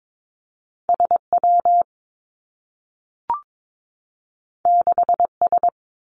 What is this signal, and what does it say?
Morse code, heard as a keyed tone, about 22 wpm, 700 Hz. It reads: S W 6 S
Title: Two Challenging Letter-Number Combinations 22wpm